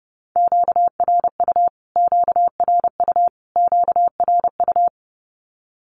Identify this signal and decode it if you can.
Morse code, heard as a keyed tone, about 30 wpm, 700 Hz. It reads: QRU QRU QRU
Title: Q-Codes Repeated 3x 30wpm